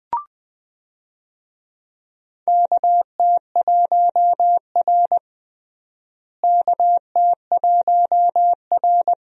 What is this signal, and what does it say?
Morse code, heard as a keyed tone, about 20 wpm, 700 Hz. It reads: KT1R KT1R
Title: USA Call Signs - 4 Characters - Repeated 2x - 20wpm